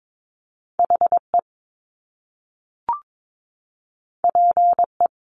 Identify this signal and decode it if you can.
Morse code, heard as a keyed tone, about 22 wpm, 700 Hz. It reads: HE PE
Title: Top Two Letter Combinations in Words 22wpm